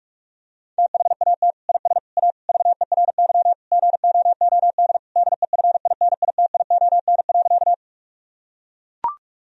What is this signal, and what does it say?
Morse code, heard as a keyed tone, about 41 wpm, 700 Hz. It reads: THAT IS A VERY GOOD DEFINITION.
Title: Sentences from Top 1000 Words - Review Part 4 - 45wpm